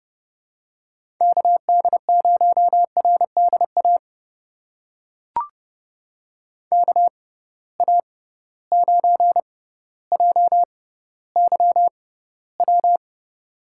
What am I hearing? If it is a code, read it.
Morse code, heard as a keyed tone, about 30 wpm, spaced as for 15 wpm, 700 Hz. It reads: KD0RDA KA9JYW
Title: USA Call Signs - 6 Characters - 6x Character Spacing - 30wpm